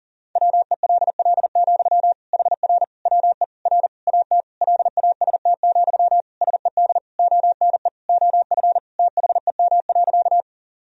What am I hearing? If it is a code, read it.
Morse code, heard as a keyed tone, about 40 wpm, 700 Hz. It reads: WELL, HR WE R AT LAST, SED ONE OF THEM.
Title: Sentences from Top 300 Words and Abbreviations 40wpm